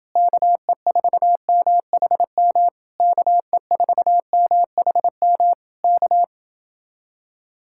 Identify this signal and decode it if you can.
Morse code, heard as a keyed tone, about 30 wpm, 700 Hz. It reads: KE4MHM KE4MHM K